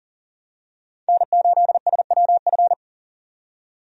Morse code, heard as a keyed tone, about 39 wpm, 700 Hz. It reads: N8SWF
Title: USA Call Signs - 5 Characters - 40wpm